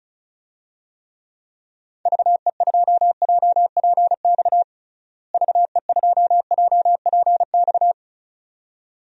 Morse code, heard as a keyed tone, about 36 wpm, 700 Hz. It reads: VE2JPX VE2JPX
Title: DXCC-100 Call Signs - 6 Characters - Repeated 2x - 35wpm